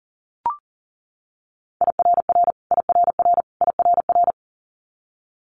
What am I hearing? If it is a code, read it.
Morse code, heard as a keyed tone, about 37 wpm, 700 Hz. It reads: IRR IRR IRR
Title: Multi-letter Phonemes Repeated 3x 40wpm